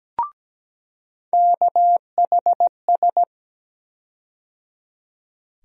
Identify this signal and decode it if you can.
Morse code, heard as a keyed tone, about 17 wpm, 700 Hz. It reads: KHS
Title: USA Call Sign Postfixes - 3 Letters 17wpm